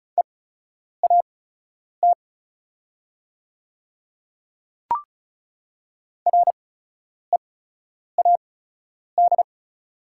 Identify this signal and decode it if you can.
Morse code, heard as a keyed tone, about 34 wpm, spaced as for 10 wpm, 700 Hz. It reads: EAT READ
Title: 8x Character Spacing - Top 300 Words - 35wpm